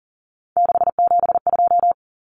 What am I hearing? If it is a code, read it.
Morse code, heard as a keyed tone, about 40 wpm, 700 Hz. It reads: 672
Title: Numbers - 3-Digit 40wpm